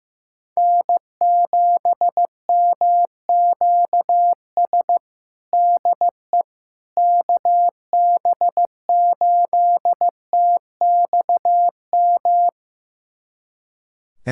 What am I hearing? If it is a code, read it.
Morse code, heard as a keyed tone, about 15 wpm, 700 Hz. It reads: N7MQS DE KB8TXM
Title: USA Callsign DE Callsign 15wpm